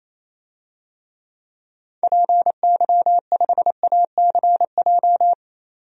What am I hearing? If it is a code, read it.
Morse code, heard as a keyed tone, about 28 wpm, 700 Hz. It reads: PY5ACJ
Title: DXCC-100 Call Signs - 6 Characters - 28wpm